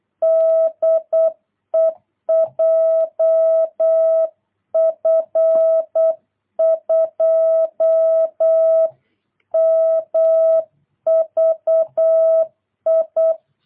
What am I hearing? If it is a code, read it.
Morse code, heard as a keyed tone, about 8 wpm, 650 Hz. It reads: DE JF2MVI
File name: jh8cbh / 2022cw